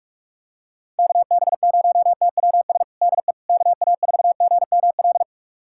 Morse code, heard as a keyed tone, about 42 wpm, 700 Hz. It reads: KB0TWS DE KA4GML
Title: USA Callsign DE Callsign - 6 Characters - 45wpm